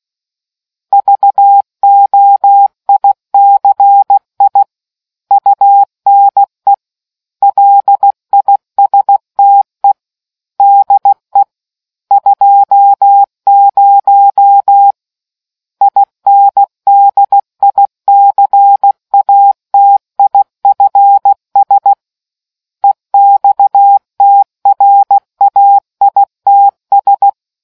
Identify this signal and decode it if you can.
Morse code, heard as a keyed tone, about 16 wpm, 800 Hz. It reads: VOICI UNE LISTE DE 20 INDICATIFS EXTRAITS